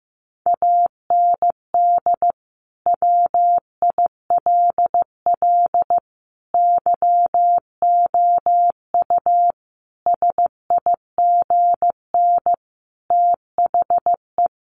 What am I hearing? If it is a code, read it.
Morse code, heard as a keyed tone, about 14 wpm, 700 Hz. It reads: AND WILL YOU SIGN THE
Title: Sentences from Top 1400 Words 15wpm